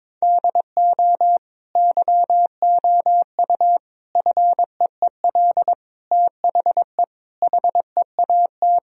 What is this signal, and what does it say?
Morse code, heard as a keyed tone, about 22 wpm, 700 Hz. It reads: DO YOU FEEL THE HEAT
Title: Sentences from Top 1000 Words - Review Part 3 - 22wpm